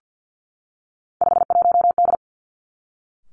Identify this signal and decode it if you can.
Morse code, heard as a keyed tone, about 50 wpm, 700 Hz. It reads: 51D
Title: Three Challenging Letter-Number Combinations 50wpm